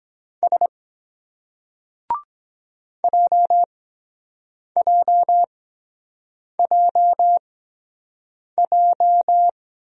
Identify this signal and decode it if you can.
Morse code, heard as a keyed tone, about 22 wpm, spaced as for 7 wpm, 700 Hz. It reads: S JJJJ